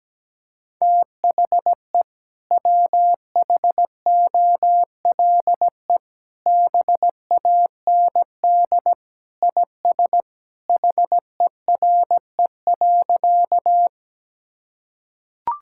Morse code, heard as a keyed tone, about 17 wpm, 700 Hz. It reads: THE WHOLE BAND IS HERE.
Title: Sentences from Top 1600 Words 17wpm